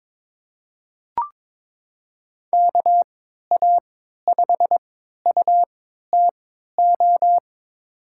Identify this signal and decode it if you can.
Morse code, heard as a keyed tone, about 22 wpm, spaced as for 12 wpm, 700 Hz. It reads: KA5UTO
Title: USA Call Signs - 6 Characters - 3x Character Spacing - 22wpm